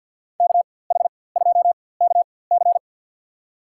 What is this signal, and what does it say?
Morse code, heard as a keyed tone, about 52 wpm, spaced as for 24 wpm, 700 Hz. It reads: KH2KC
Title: USA Call Signs - 5 Characters - 4x Character Spacing - 50wpm